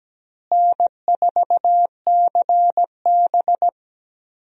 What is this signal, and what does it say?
Morse code, heard as a keyed tone, about 17 wpm, 700 Hz. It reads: N4CB
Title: USA Call Signs - 4 Characters - Repeated 3x - 17wpm